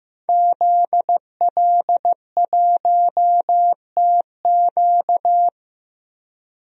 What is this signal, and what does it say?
Morse code, heard as a keyed tone, about 16 wpm, 700 Hz. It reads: ZL1TQ
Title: DXCC-100 Call Signs - 5 Characters - 15wpm